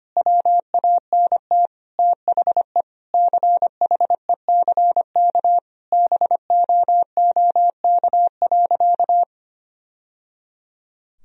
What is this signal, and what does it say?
Morse code, heard as a keyed tone, about 25 wpm, 700 Hz. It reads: WANT THE CHECK BOOK.
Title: Sentences from Top 1000 Words - Review Part 3 - 25wpm